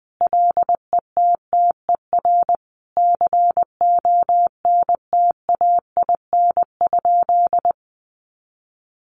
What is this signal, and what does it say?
Morse code, heard as a keyed tone, about 20 wpm, 700 Hz. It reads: LETTER CONTAIN?
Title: Sentences from Top 500 Words 20wpm